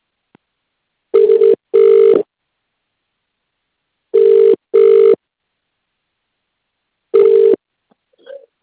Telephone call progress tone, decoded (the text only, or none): ringback tone